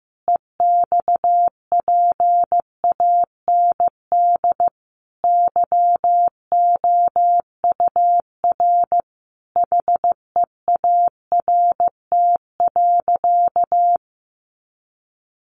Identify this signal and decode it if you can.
Morse code, heard as a keyed tone, about 15 wpm, 700 Hz. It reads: EXPAND YOUR HEART.
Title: Sentences from Top 1400 Words 15wpm